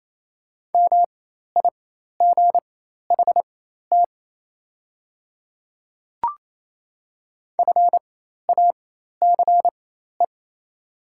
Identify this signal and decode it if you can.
Morse code, heard as a keyed tone, about 27 wpm, spaced as for 13 wpm, 690 Hz. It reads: MIGHT FACE
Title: 4x Character Spacing - Top 300 Words - 28wpm